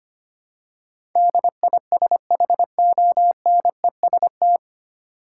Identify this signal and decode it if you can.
Morse code, heard as a keyed tone, about 25 wpm, 700 Hz. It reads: DISHONEST
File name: practice / 25wpm / 9 Letter Words - First 3 Letters Given - 25wpm